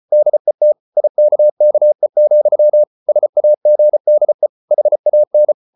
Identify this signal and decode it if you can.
Morse code, heard as a keyed tone, about 34 wpm, 600 Hz. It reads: DET IKKE, SAGDE HAN
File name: Ferie QRQ 34wpm